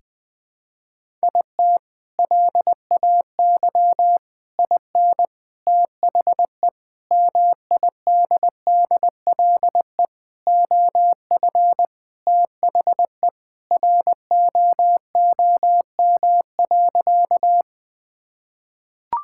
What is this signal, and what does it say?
Morse code, heard as a keyed tone, about 20 wpm, 700 Hz. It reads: IT LAY IN THE MIDDLE OF THE ROOM.
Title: Sentences from Top 1000 Words - Review Part 6 - 20wpm